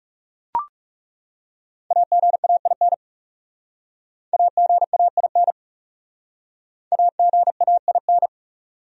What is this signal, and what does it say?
Morse code, heard as a keyed tone, about 37 wpm, 700 Hz. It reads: AGAIN AGAIN AGAIN
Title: Speed-Racing Top 300 Words 30wpm